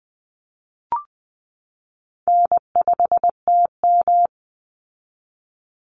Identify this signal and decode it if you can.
Morse code, heard as a keyed tone, about 20 wpm, 700 Hz. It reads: N5TM
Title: Easy Call Signs Repeated 3x - 20wpm